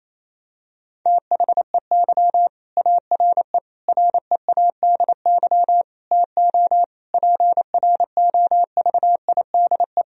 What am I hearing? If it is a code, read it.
Morse code, heard as a keyed tone, about 27 wpm, 700 Hz. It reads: THEY ARE READY TO PROVIDE
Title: Sentences from Top 1000 Words - Review Part 3 - 28wpm